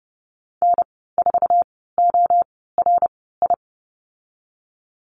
Morse code, heard as a keyed tone, about 30 wpm, spaced as for 17 wpm, 700 Hz. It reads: N4ORI
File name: Lesson 15g - Number 5 - Call Signs 30wpm